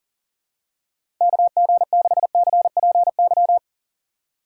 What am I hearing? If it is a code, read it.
Morse code, heard as a keyed tone, about 40 wpm, 700 Hz. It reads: KG6CPY
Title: USA Call Signs - 6 Characters - 2x Character Spacing - 40wpm